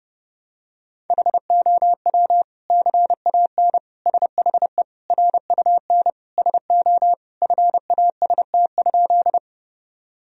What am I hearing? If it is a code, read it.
Morse code, heard as a keyed tone, about 30 wpm, 700 Hz. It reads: HOW CAN SHE RUN SO FAST?
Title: Sentences from Top 1000 Words - Review Part 2 - 30wpm